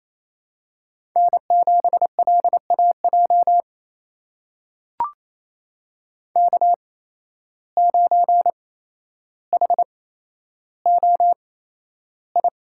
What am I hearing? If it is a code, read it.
Morse code, heard as a keyed tone, about 27 wpm, spaced as for 12 wpm, 700 Hz. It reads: N7LAJ K9HOI
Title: USA Call Signs - 5 Characters - 8x Character Spacing - 28wpm